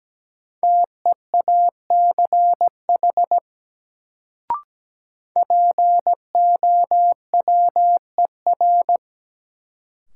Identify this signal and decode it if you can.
Morse code, heard as a keyed tone, about 17 wpm, 700 Hz. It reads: TEACH POWER
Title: Sentences from Top 500 Words - New Word Drill - 17wpm